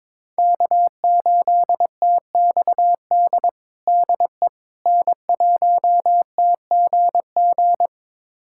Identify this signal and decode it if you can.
Morse code, heard as a keyed tone, about 22 wpm, 700 Hz. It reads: K8TXD DE N1TGG
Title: USA Callsign DE Callsign - 5 Characters - 22wpm